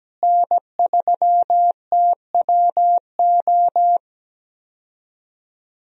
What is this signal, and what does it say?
Morse code, heard as a keyed tone, about 17 wpm, 700 Hz. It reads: N3TWO
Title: USA Call Signs - 5 Characters - Repeated 2x - 17wpm